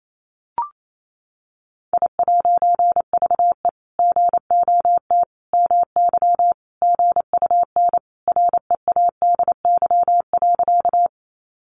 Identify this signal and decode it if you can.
Morse code, heard as a keyed tone, about 28 wpm, 700 Hz. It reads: I'VE GOT MY GUN READY.
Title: A Sound Of Thunder 28wpm